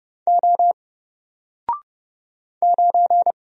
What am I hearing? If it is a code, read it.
Morse code, heard as a keyed tone, about 29 wpm, 700 Hz. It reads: O 9
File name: Single-Letter-Number-30wpm